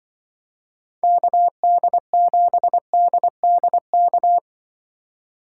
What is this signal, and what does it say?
Morse code, heard as a keyed tone, about 24 wpm, 700 Hz. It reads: KD7DDK
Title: Speed Racing - USA 2x3 Call Signs 20wpm